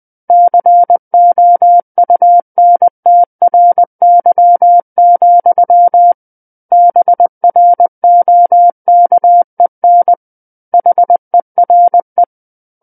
Morse code, heard as a keyed tone, about 20 wpm, 700 Hz. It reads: COUNTRY, BROKEN HERE
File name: 20 WPM